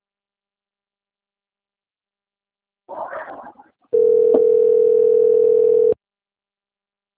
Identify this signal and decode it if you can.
Ringback tone (North America), 440 + 480 Hz.